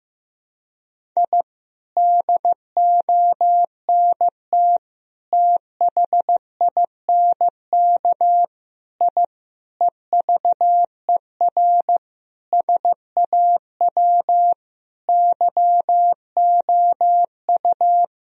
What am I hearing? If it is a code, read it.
Morse code, heard as a keyed tone, about 16 wpm, 700 Hz. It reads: I DONT THINK I EVER SAW YOU
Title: Sentences from Top 1000 Words - Review Part 4 - 15wpm